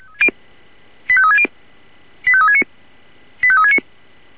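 Five-tone selective call (ZVEI), heard as quick runs of five tones, tones of about 70 ms. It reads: 85360 85369 85369
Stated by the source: ZVEI five-tone selective call received by SDR.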